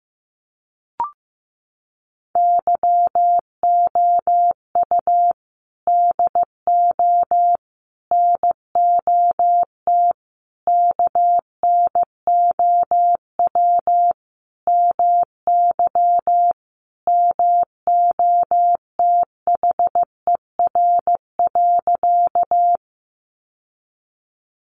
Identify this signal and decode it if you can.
Morse code, heard as a keyed tone, about 15 wpm, 700 Hz. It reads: YOU DO NOT KNOW MY MOTHER.
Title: Sentences from Top 600 Words 15wpm